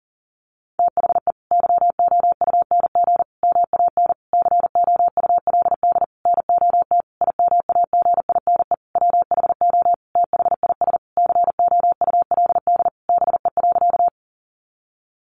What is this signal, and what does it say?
Morse code, heard as a keyed tone, about 40 wpm, 700 Hz. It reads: THE YOUNG MAN COULD NOT IMAGINE WHO THIS COULD BE.
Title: Sentences from Top 1400 Words 40wpm